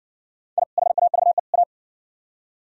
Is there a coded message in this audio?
Yes: Morse code from 0.6 to 1.6 s, heard as a keyed tone, about 57 wpm, 700 Hz. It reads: I HAVE A